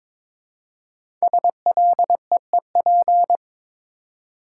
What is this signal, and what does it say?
Morse code, heard as a keyed tone, about 22 wpm, 700 Hz. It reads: SLEEP
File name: 5 Letter Words - First 3 Letters Given - 22wpm